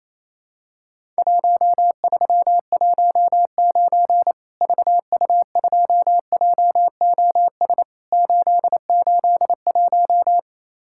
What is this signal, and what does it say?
Morse code, heard as a keyed tone, about 28 wpm, 700 Hz. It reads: 1319 VU2JOS 881
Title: Worked All Europe DX Contest - QTC Exchange - 28wpm